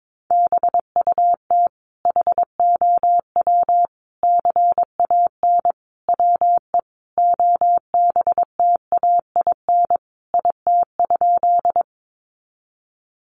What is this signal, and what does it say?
Morse code, heard as a keyed tone, about 22 wpm, 700 Hz. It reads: BUT HOW CAN WE OBTAIN IT?
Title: Sentences from Top 1300 Words 22wpm